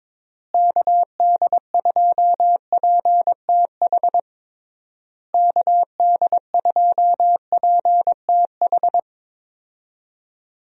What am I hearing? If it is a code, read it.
Morse code, heard as a keyed tone, about 22 wpm, 700 Hz. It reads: KD2PTH KD2PTH
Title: USA Call Signs - 6 Characters - Repeated 3x - 22wpm